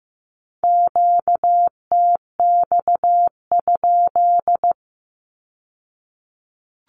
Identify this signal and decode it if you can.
Morse code, heard as a keyed tone, about 15 wpm, 700 Hz. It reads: QTX?
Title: Q-Codes with Spoken Meaning - 15wpm